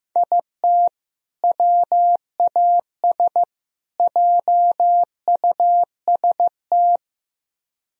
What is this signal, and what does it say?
Morse code, heard as a keyed tone, about 16 wpm, 700 Hz. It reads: IT WAS JUST
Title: Speed-Racing - Sets of 3 Words 15wpm